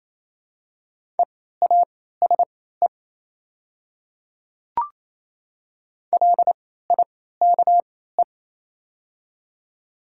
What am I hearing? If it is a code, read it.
Morse code, heard as a keyed tone, about 27 wpm, spaced as for 16 wpm, 700 Hz. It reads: EASE LIKE
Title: 3x Character Spacing - Top 500 Words - 28wpm